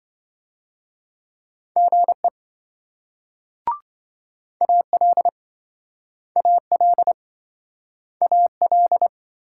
Morse code, heard as a keyed tone, about 26 wpm, 700 Hz. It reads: GE AL AL AL